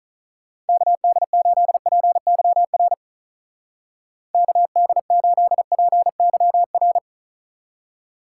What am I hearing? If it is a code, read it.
Morse code, heard as a keyed tone, about 44 wpm, 700 Hz. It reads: KD8PYR KD8PYR